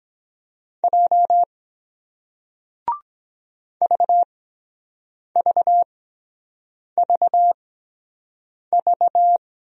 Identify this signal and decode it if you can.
Morse code, heard as a keyed tone, about 23 wpm, spaced as for 7 wpm, 700 Hz. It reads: J VVVV